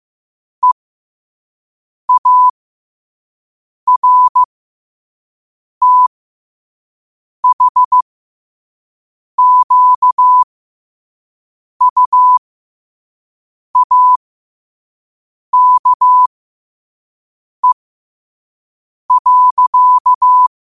Morse code, heard as a keyed tone, about 15 wpm, spaced as for 5 wpm, 1000 Hz. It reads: EARTHQUAKE.
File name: qotdfiles / 5wpm